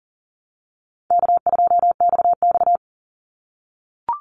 Morse code, heard as a keyed tone, about 41 wpm, 700 Hz. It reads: K2XX
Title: Easy Call Signs - 40wpm